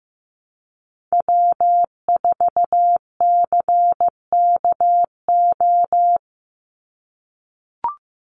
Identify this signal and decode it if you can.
Morse code, heard as a keyed tone, about 15 wpm, 700 Hz. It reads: W4CKO
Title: USA Call Signs 15wpm